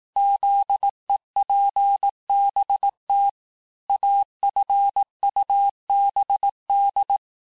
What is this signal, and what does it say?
Morse code, heard as a keyed tone, about 18 wpm, 800 Hz. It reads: ZEPBT AFUBD
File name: cours-du-lundi files / cours-1852-texte-chiffre-vitesse-17MPM